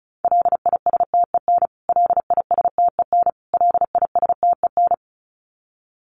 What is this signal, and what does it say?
Morse code, heard as a keyed tone, about 34 wpm, 700 Hz. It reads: LISTEN LISTEN LISTEN
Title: Top 500 Words Repeated 35wpm